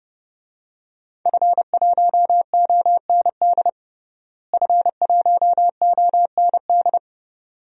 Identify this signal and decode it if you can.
Morse code, heard as a keyed tone, about 30 wpm, 700 Hz. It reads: F1OND F1OND
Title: DXCC-100 Call Signs - 5 Characters - Repeated 2x - 30wpm